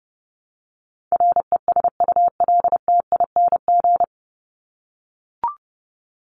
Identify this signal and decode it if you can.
Morse code, heard as a keyed tone, about 30 wpm, 700 Hz. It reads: RESULTING
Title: ING Suffix - Top 100 Words with Base Words 30wpm